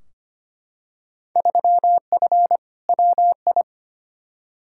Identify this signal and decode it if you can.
Morse code, heard as a keyed tone, about 25 wpm, 700 Hz. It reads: 3F WI
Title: Field Day Exchanges - 25wpm